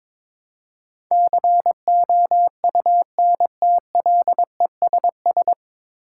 Morse code, heard as a keyed tone, about 22 wpm, 700 Hz. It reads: COUNTLESS
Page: Morse Code Ninja practice 22wpm